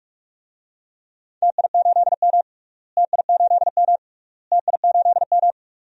Morse code, heard as a keyed tone, about 46 wpm, 700 Hz. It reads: TI8M TI8M TI8M
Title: DXCC-100 Call Signs - 4 Characters - Repeated 3x - 45wpm